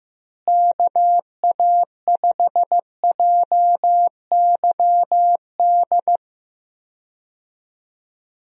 Morse code, heard as a keyed tone, about 16 wpm, 700 Hz. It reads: KA5JYD
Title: USA Call Signs - 6 Characters - Repeated 2x - 15wpm